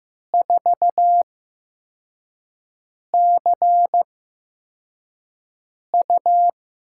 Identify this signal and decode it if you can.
Morse code, heard as a keyed tone, about 15 wpm, spaced as for 4 wpm, 700 Hz. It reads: 4CU
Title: USA Call Signs - 4 Characters - 8x Character Spacing - 15wpm